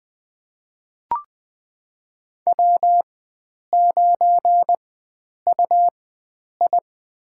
Morse code, heard as a keyed tone, about 20 wpm, spaced as for 9 wpm, 700 Hz. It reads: W9UI
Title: USA Call Signs - 4x Character Spacing - 20wpm